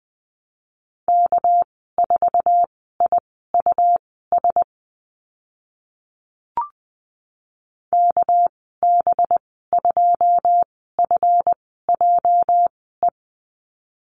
Morse code, heard as a keyed tone, about 20 wpm, spaced as for 15 wpm, 700 Hz. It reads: K4IUS KB2FJE
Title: 219 Lesson 35g - Number 0 - Call Signs 20wpm